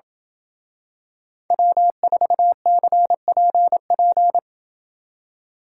Morse code, heard as a keyed tone, about 27 wpm, 700 Hz. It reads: W4CPP